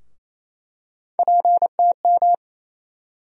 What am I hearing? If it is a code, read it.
Morse code, heard as a keyed tone, about 28 wpm, 700 Hz. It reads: PTM
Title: USA Call Sign Postfixes - 3 Letters 28wpm